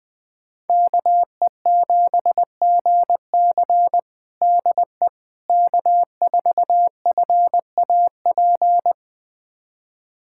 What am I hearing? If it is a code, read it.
Morse code, heard as a keyed tone, about 20 wpm, 700 Hz. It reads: KE7GC DE K4FAP
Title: USA Callsign DE Callsign 20wpm